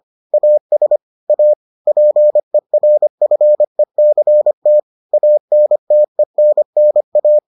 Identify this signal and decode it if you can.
Morse code, heard as a keyed tone, about 25 wpm, 600 Hz. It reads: AS A PERFECT ANTENNA